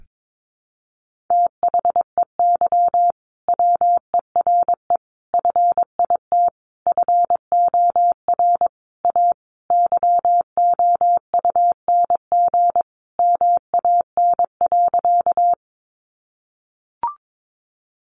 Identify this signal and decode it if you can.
Morse code, heard as a keyed tone, about 22 wpm, 700 Hz. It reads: THEY WERE FIT FOR A YOUNG MAN.
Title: Sentences from Top 800 Words 22wpm